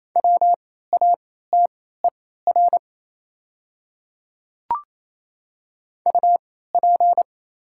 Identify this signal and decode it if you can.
Morse code, heard as a keyed tone, about 28 wpm, spaced as for 16 wpm, 700 Hz. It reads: WATER UP
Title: 3x Character Spacing - Top 100 Words - 28wpm